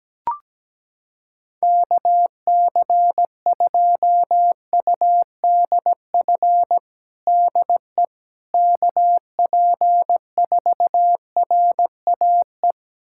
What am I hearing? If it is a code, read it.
Morse code, heard as a keyed tone, about 17 wpm, 700 Hz. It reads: KC2UDF DE KP4RAE
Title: USA Callsign DE Callsign - 6 Characters - 17wpm